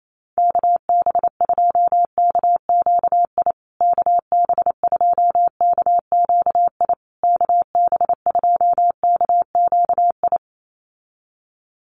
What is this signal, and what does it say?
Morse code, heard as a keyed tone, about 28 wpm, 700 Hz. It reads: KB2KQI KB2KQI KB2KQI
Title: USA Call Signs Repeated 3x 28wpm